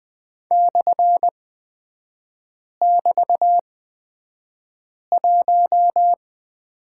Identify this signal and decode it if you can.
Morse code, heard as a keyed tone, about 20 wpm, spaced as for 5 wpm, 700 Hz. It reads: /<BT>1
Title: 250 Lesson 40e - BK - Three Characters 20wpm